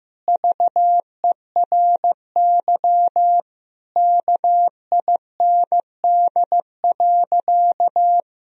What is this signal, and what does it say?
Morse code, heard as a keyed tone, about 15 wpm, 700 Hz. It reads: VERY KIND.
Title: Sentences from Top 1000 Words - Review Part 3 - 15wpm